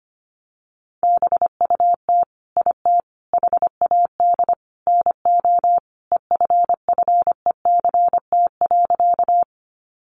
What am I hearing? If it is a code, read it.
Morse code, heard as a keyed tone, about 25 wpm, 700 Hz. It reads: BUT IT HAD NO EFFECT.
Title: Sentences from Top 1000 Words 25wpm